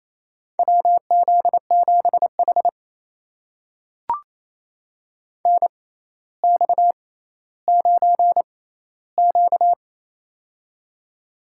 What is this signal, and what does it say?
Morse code, heard as a keyed tone, about 27 wpm, spaced as for 14 wpm, 700 Hz. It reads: WZ7H NX9Q
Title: USA Call Signs - 4 Characters - 6x Character Spacing - 28wpm